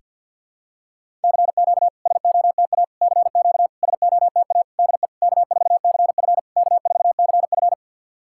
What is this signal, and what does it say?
Morse code, heard as a keyed tone, about 50 wpm, 690 Hz. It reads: CQ SOTA CQ SOTA DE K4CF K4CF
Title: SOTA QSO Practice 50wpm